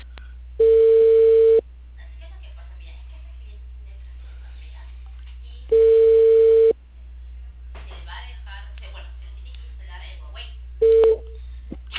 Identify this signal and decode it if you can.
Ringback tone (China, 450 Hz), with speech.